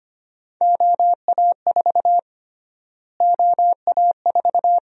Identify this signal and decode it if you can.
Morse code, heard as a keyed tone, about 25 wpm, 700 Hz. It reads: OA4 OA4
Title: USA Call Sign Prefixes - Repeated 2x - 25wpm